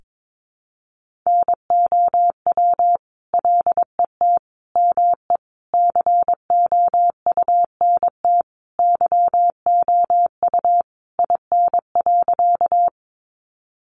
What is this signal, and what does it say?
Morse code, heard as a keyed tone, about 22 wpm, 700 Hz. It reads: NOW LET ME COUNT YOU IN.
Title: Sentences from Top 1500 Words 22wpm